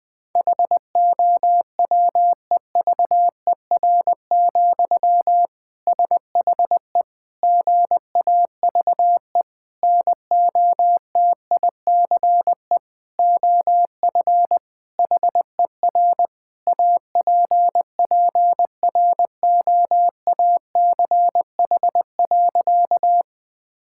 Morse code, heard as a keyed tone, about 20 wpm, 700 Hz. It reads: HOWEVER, SHE GAVE NOTICE OF HER APPROACH.
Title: Sentences from Top 1000 Words - Review Part 6 - 20wpm